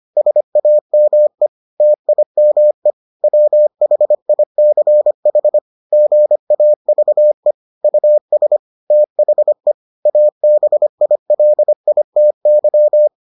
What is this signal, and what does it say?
Morse code, heard as a keyed tone, about 25 wpm, 600 Hz. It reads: SAME TIME WHICH GAVE US THE ABILITY